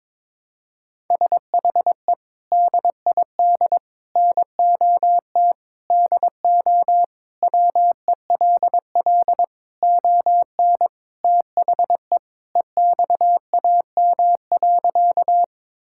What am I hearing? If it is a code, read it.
Morse code, heard as a keyed tone, about 22 wpm, 700 Hz. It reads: SHE DID NOT DO WELL ON THE EXAM.
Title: Sentences from Top 1000 Words - Review Part 5 - 22wpm